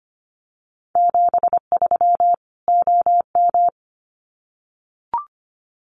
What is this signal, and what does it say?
Morse code, heard as a keyed tone, about 25 wpm, 700 Hz. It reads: 73 OM
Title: Common QSO Phrases 25wpm